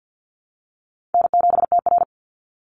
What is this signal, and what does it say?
Morse code, heard as a keyed tone, about 50 wpm, 700 Hz. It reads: N7TR
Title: Easy Call Signs - 50wpm